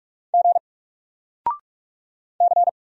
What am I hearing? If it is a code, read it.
Morse code, heard as a keyed tone, about 44 wpm, 700 Hz. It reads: G C
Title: Single Letter-Number Alt 45wpm